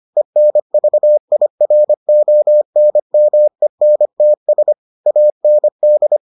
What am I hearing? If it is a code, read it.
Morse code, heard as a keyed tone, about 25 wpm, 600 Hz. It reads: ENVIRONMENTS AND